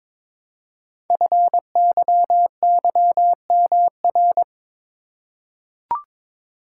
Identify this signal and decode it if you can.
Morse code, heard as a keyed tone, about 22 wpm, 700 Hz. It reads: FYYMR
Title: Code Groups - 5 Letter:Numbers - 22wpm